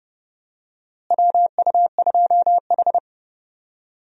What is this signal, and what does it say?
Morse code, heard as a keyed tone, about 32 wpm, 700 Hz. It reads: WU2H
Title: USA Call Signs - 4 Characters - Repeated 3x - 30wpm